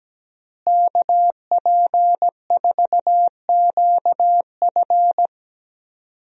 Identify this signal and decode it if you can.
Morse code, heard as a keyed tone, about 17 wpm, 700 Hz. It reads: KP4QF
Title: DXCC-100 Call Signs - 5 Characters - Repeated 2x - 17wpm